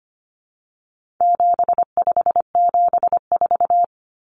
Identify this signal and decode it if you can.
Morse code, heard as a keyed tone, about 25 wpm, 700 Hz. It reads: 7574
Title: Common Cut Numbers - 4-Digit 25wpm